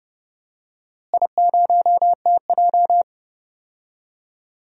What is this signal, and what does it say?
Morse code, heard as a keyed tone, about 30 wpm, 700 Hz. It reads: I0TJ
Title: DXCC-100 Call Signs - 4 Characters - Repeated 3x - 30wpm